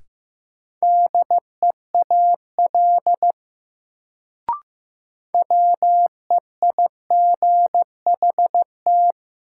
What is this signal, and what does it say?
Morse code, heard as a keyed tone, about 15 wpm, 700 Hz. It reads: DEAL WEIGHT
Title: Sentences from Top 800 Words - New Word Drill - 15wpm